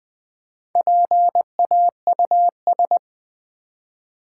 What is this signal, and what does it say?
Morse code, heard as a keyed tone, about 20 wpm, 700 Hz. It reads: PAUS
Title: Build A Word - 1 Letter At A Time - 3 to 5-Letter Words - 20wpm